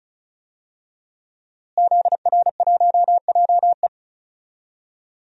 Morse code, heard as a keyed tone, about 34 wpm, 690 Hz. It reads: ZR1JE
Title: DXCC-100 Call Signs - 5 Characters - Repeated 3x - 35wpm